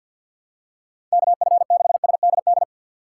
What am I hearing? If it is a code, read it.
Morse code, heard as a keyed tone, about 50 wpm, 700 Hz. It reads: KF6SDD
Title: USA Call Signs - 2x Character Spacing - 50wpm